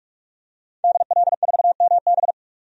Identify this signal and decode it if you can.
Morse code, heard as a keyed tone, about 46 wpm, 700 Hz. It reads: DL4MB
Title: DXCC-100 Call Signs - 5 Characters - 45wpm